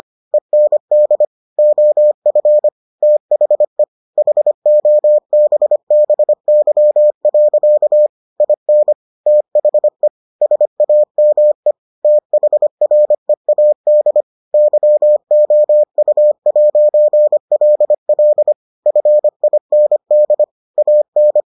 Morse code, heard as a keyed tone, about 25 wpm, 600 Hz. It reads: END OF THE HOBBY. IN THE SAME THREAD YOU'LL FIND AN